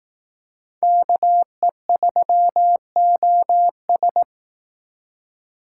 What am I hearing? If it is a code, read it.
Morse code, heard as a keyed tone, about 18 wpm, 700 Hz. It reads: KE3OS
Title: Speed Racing - USA 2x2 Call Signs 15wpm